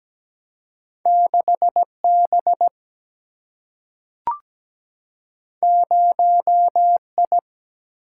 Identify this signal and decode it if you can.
Morse code, heard as a keyed tone, about 17 wpm, 700 Hz. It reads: 6 B 0 I
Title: Single-Letter Single-Number 17wpm